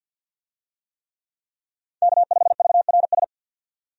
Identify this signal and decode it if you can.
Morse code, heard as a keyed tone, about 50 wpm, 700 Hz. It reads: K5VRS
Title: USA Call Signs - 5 Characters - 50wpm